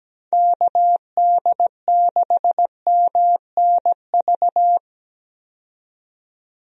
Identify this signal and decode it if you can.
Morse code, heard as a keyed tone, about 17 wpm, 700 Hz. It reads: KD6MNV
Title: USA Call Signs - 6 Characters - 17wpm